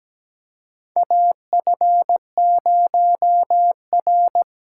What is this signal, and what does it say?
Morse code, heard as a keyed tone, about 17 wpm, 700 Hz. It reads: AF0R
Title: USA Call Signs - 4 Characters - 4x Character Spacing - 17wpm